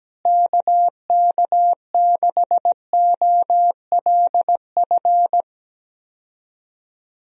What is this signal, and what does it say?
Morse code, heard as a keyed tone, about 17 wpm, 700 Hz. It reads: KK6OLF
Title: USA Call Signs - 6 Characters - Repeated 3x - 17wpm